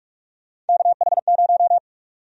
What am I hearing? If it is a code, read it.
Morse code, heard as a keyed tone, about 44 wpm, 700 Hz. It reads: KH0
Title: USA Call Sign Prefixes - Repeated 2x - 45wpm